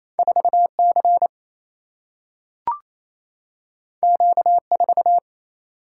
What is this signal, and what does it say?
Morse code, heard as a keyed tone, about 28 wpm, 700 Hz. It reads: 4C Q4
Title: Single-Letter Single-Number 28wpm